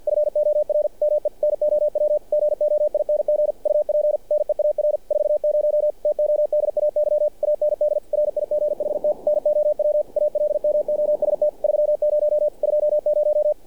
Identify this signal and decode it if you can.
Morse code, heard as a keyed tone, about 48 wpm, 600 Hz. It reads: FOR ME NOW GOING UP NEAR 40 TODAY AND RIGHT NOW ALMOST 30 20
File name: archived / CRSnet